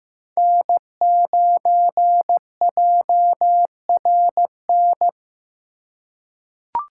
Morse code, heard as a keyed tone, about 15 wpm, 700 Hz. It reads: N9JRN
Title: USA Call Signs - 5 Characters - 4x Character Spacing - 15wpm